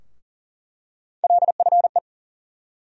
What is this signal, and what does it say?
Morse code, heard as a keyed tone, about 40 wpm, 700 Hz. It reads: LFE
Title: USA Call Sign Postfixes - 3 Letters 40wpm